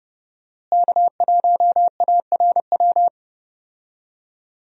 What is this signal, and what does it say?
Morse code, heard as a keyed tone, about 32 wpm, 700 Hz. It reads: K1ARW
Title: USA Call Signs - 5 Characters - Repeated 3x - 30wpm